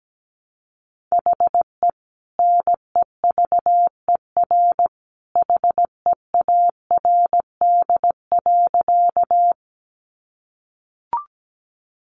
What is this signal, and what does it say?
Morse code, heard as a keyed tone, about 17 wpm, 700 Hz. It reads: HE NEVER HEARD.
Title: Sentences from Top 400 Words 17wpm